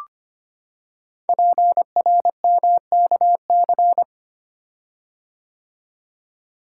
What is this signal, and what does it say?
Morse code, heard as a keyed tone, about 25 wpm, 700 Hz. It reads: PRMKC
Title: Code Groups - 5 Letters - 25wpm